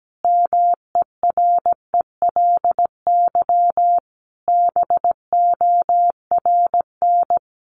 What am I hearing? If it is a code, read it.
Morse code, heard as a keyed tone, about 17 wpm, 700 Hz. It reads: MERELY BORN
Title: Sentences from Top 1400 Words 17wpm